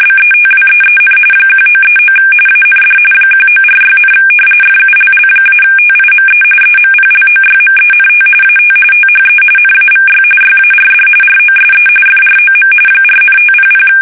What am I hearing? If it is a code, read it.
RTTY, about 45.45 baud. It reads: BETWEEN 1939 AND 1946, FULL-TIME EMPLOYMENT AT THE NAVY YARD ROSE FROM 4,500
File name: fsk850 / battleshipnj